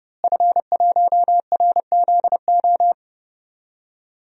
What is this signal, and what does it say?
Morse code, heard as a keyed tone, about 30 wpm, 700 Hz. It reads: F1RZO
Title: DXCC-100 Call Signs - 5 Characters - Repeated 2x - 30wpm